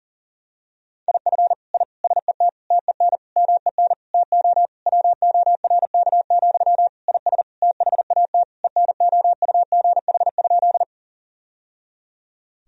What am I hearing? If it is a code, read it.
Morse code, heard as a keyed tone, about 40 wpm, 700 Hz. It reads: IF I SET TEN MEN TO WORK, IS THAT ENOUGH?
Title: Sentences from Top 1000 Words - Review Part 2 - 40wpm